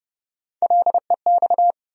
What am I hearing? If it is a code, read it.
Morse code, heard as a keyed tone, about 30 wpm, 700 Hz. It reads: LEX
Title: USA Call Sign Postfixes - 3 Letters 30wpm